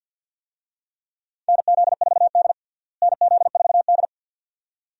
Morse code, heard as a keyed tone, about 51 wpm, 700 Hz. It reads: NZ4D NZ4D
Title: USA Call Signs - 4 Characters - Repeated 2x - 50wpm